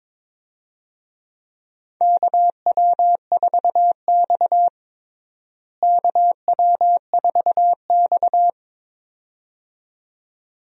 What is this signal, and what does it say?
Morse code, heard as a keyed tone, about 22 wpm, 700 Hz. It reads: KW4X KW4X
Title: USA Call Signs - 4 Characters - Repeated 2x - 22wpm